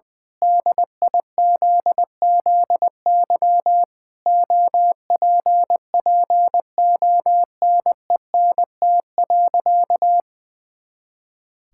Morse code, heard as a keyed tone, about 20 wpm, 700 Hz. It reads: DIZZY OPPONENT.
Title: Pangram Challenge v2 - 20wpm